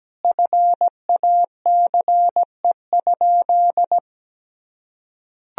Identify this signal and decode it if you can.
Morse code, heard as a keyed tone, about 17 wpm, 700 Hz. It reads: FACE?
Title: Sentences from Top 1000 Words - Review Part 2 - 17wpm